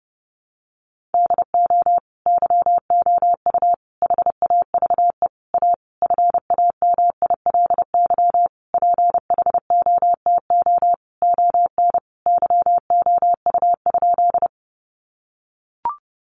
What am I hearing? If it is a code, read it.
Morse code, heard as a keyed tone, about 30 wpm, 700 Hz. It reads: DO YOU HAVE A FAMILY PHOTO ON YOU?
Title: Sentences from Top 1200 Words 30wpm